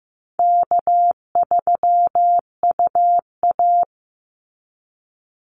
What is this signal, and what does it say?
Morse code, heard as a keyed tone, about 15 wpm, 700 Hz. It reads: K3UA
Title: Call Signs Repeated 3x - 15wpm